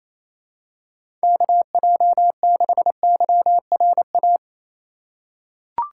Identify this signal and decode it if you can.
Morse code, heard as a keyed tone, about 28 wpm, 700 Hz. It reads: KJ6YRA
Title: USA Call Signs - 6 Characters - 2x Character Spacing - 28wpm